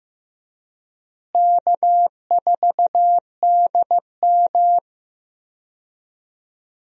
Morse code, heard as a keyed tone, about 16 wpm, 700 Hz. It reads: K4DM
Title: USA Call Signs - 4 Characters - Repeated 2x - 15wpm